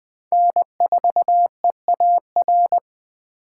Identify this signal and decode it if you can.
Morse code, heard as a keyed tone, about 20 wpm, 700 Hz. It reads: N4EAR
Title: Speed Racing - USA 1x3 Call Signs 17wpm